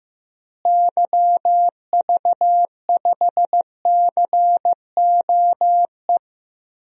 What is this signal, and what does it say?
Morse code, heard as a keyed tone, about 15 wpm, 700 Hz. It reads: YV5COE
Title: DXCC-100 Call Signs - 6 Characters - Repeated 3x - 15wpm